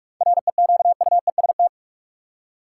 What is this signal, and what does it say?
Morse code, heard as a keyed tone, about 45 wpm, 700 Hz. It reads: REQUEST
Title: Seven Letter Words - 45wpm